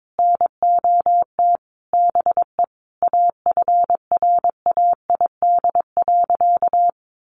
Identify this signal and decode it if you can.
Morse code, heard as a keyed tone, about 22 wpm, 700 Hz. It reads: NOT BE AFRAID.
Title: Sentences from Top 1500 Words 22wpm